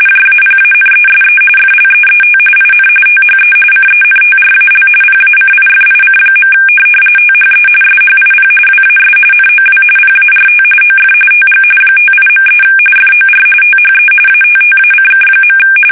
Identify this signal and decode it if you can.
RTTY, about 45.45 baud. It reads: JAPANESE FLEET IN THE CENTRAL PACIFIC. THE SHIP WAS NOW PART OF TG 58.3 FOR THE ATTACK.
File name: fsk850 / USS-Massachusetts